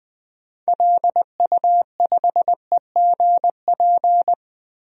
Morse code, heard as a keyed tone, about 20 wpm, 700 Hz. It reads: LU5EGP
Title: DXCC-100 Call Signs - 6 Characters - 20wpm